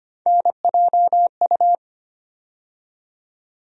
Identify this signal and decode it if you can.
Morse code, heard as a keyed tone, about 25 wpm, 700 Hz. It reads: NJU
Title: USA Call Sign Postfixes - 3 Letters 25wpm